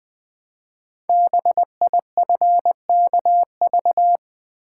Morse code, heard as a keyed tone, about 20 wpm, 700 Hz. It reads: BIFKV
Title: Code Groups - 5 Letter:Numbers - 20wpm